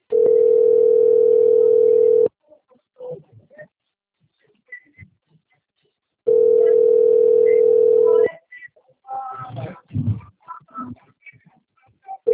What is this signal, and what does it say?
Ringback tone (North America), 440 + 480 Hz.